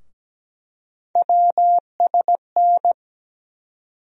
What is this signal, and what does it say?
Morse code, heard as a keyed tone, about 17 wpm, 700 Hz. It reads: WSN
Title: USA Call Sign Postfixes - 3 Letters 17wpm